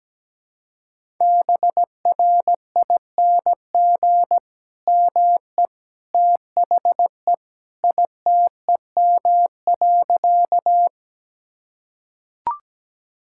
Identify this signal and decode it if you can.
Morse code, heard as a keyed tone, about 17 wpm, 700 Hz. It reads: BRING ME THE ITEM.
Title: Sentences from Top 1000 Words - Review Part 6 - 17wpm